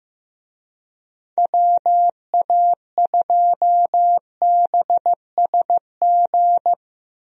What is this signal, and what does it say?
Morse code, heard as a keyed tone, about 15 wpm, 700 Hz. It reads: WA2BSG
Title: USA Call Signs - 6 Characters - Repeated 2x - 15wpm